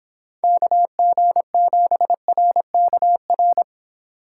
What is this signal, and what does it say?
Morse code, heard as a keyed tone, about 29 wpm, 700 Hz. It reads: KG7RKR